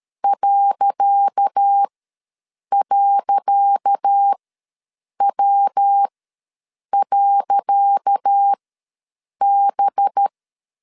Morse code, heard as a keyed tone, about 13 wpm, spaced as for 7 wpm, 800 Hz. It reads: ..W.B
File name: LEZIONI / 10 SIGNS / 10C DOT ws 12wpm 3548